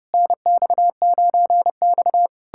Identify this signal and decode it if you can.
Morse code, heard as a keyed tone, about 33 wpm, 700 Hz. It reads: NX9X